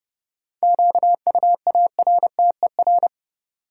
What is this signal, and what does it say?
Morse code, heard as a keyed tone, about 30 wpm, 700 Hz. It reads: QUARTER
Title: Sentences from Top 1300 Words - New Word Drill Alt - 30wpm